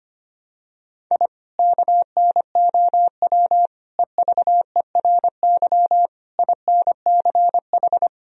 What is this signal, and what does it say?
Morse code, heard as a keyed tone, about 25 wpm, 700 Hz. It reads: I KNOW EVERY INCH
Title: Sentences from Top 1000 Words - Review Part 3 - 25wpm